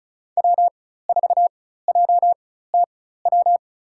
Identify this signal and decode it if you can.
Morse code, heard as a keyed tone, about 34 wpm, spaced as for 16 wpm, 700 Hz. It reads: W4JTW
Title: USA Call Signs - 5 Characters - 4x Character Spacing - 35wpm